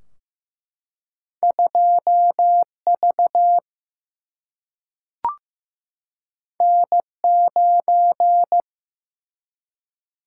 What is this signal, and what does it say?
Morse code, heard as a keyed tone, about 15 wpm, 700 Hz. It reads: 2V N9
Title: Single-Letter Single-Number 15wpm